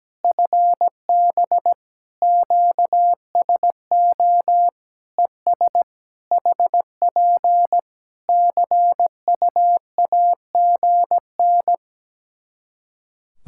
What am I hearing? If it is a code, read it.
Morse code, heard as a keyed tone, about 17 wpm, 700 Hz. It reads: FB QSO ES HP CUAGN
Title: Practice QSO - Exchange 4 - Ending - 17wpm